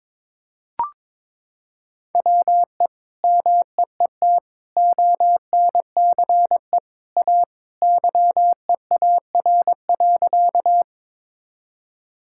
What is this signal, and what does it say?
Morse code, heard as a keyed tone, about 22 wpm, 700 Hz. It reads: WE MEET ONCE A YEAR.
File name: practice / 22wpm / Sentences with 2-Word Phrases - 22wpm